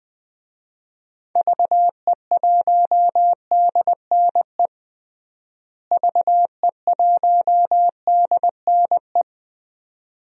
Morse code, heard as a keyed tone, about 20 wpm, 690 Hz. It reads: VE1DNE VE1DNE
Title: DXCC-100 Call Signs - 6 Characters - Repeated 2x - 20wpm